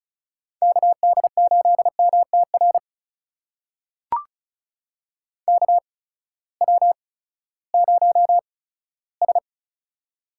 Morse code, heard as a keyed tone, about 36 wpm, spaced as for 18 wpm, 700 Hz. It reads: KD8MTR KW0S
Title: USA Call Signs - 8x Character Spacing - 35wpm